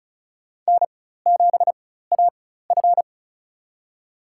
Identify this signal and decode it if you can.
Morse code, heard as a keyed tone, about 36 wpm, spaced as for 16 wpm, 700 Hz. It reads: N7AF
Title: USA Call Signs - 4 Characters - 4x Character Spacing - 35wpm